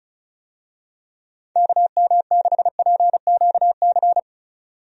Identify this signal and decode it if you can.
Morse code, heard as a keyed tone, about 34 wpm, 700 Hz. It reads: KM6PQC
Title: USA Call Signs - 6 Characters - 35wpm